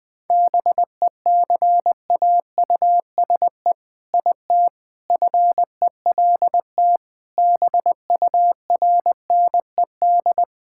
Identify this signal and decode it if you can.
Morse code, heard as a keyed tone, about 20 wpm, 700 Hz. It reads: BECAUSE IT FELT BURNED
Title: All-Age Appropriate Jokes - 20wpm